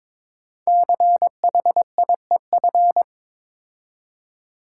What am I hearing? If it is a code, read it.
Morse code, heard as a keyed tone, about 23 wpm, 700 Hz. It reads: CHIEF
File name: 5 Letter Words - First 3 Letters Given - 22wpm